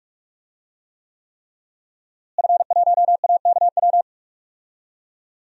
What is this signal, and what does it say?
Morse code, heard as a keyed tone, about 46 wpm, 700 Hz. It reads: F1AKW
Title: DXCC-100 Call Signs - 5 Characters - Repeated 3x - 45wpm